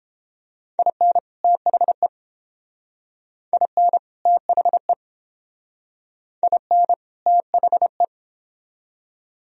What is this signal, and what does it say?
Morse code, heard as a keyed tone, about 30 wpm, 700 Hz. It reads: IN THE IN THE IN THE